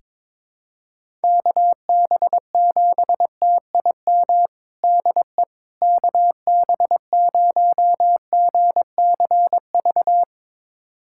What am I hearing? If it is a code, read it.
Morse code, heard as a keyed tone, about 22 wpm, 700 Hz. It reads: KB7TIM DE KB0GCV
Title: USA Callsign DE Callsign - 6 Characters - 22wpm